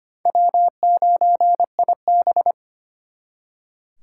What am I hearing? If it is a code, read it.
Morse code, heard as a keyed tone, about 32 wpm, 700 Hz. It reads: W9IB